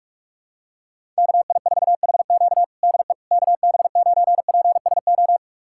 Morse code, heard as a keyed tone, about 43 wpm, 700 Hz. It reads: KI4HQ DE KB9PSO
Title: USA Callsign DE Callsign 45wpm